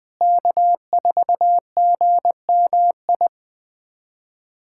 Morse code, heard as a keyed tone, about 21 wpm, 700 Hz. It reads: K4GMI
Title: USA Callsign DE Callsign 20wpm